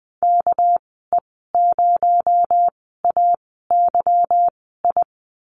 Morse code, heard as a keyed tone, about 20 wpm, spaced as for 14 wpm, 700 Hz. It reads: KE0AYI
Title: 245 Lesson 39g - Letter z - Call Signs 20wpm